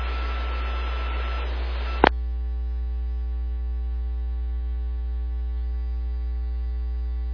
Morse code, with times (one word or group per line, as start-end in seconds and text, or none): none